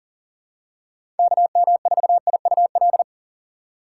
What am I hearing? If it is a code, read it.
Morse code, heard as a keyed tone, about 38 wpm, 700 Hz. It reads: KM4IUL
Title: USA Call Signs - 6 Characters - 4x Character Spacing - 40wpm